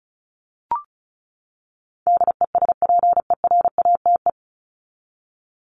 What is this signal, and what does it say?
Morse code, heard as a keyed tone, about 34 wpm, 700 Hz. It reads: DESPERATE
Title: Top 1751-2000 Words 35wpm